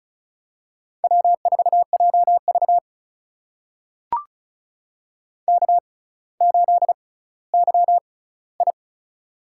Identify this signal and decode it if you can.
Morse code, heard as a keyed tone, about 36 wpm, spaced as for 18 wpm, 700 Hz. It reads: W4JV K8YI
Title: USA Call Signs - 4 Characters - 6x Character Spacing - 35wpm